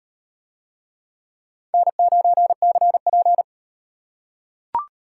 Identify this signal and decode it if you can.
Morse code, heard as a keyed tone, about 38 wpm, 700 Hz. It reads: N9CP